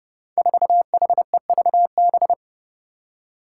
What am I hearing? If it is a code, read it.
Morse code, heard as a keyed tone, about 32 wpm, 700 Hz. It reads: 4HEVB
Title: Five Challenging Letter-Number Combinations 30wpm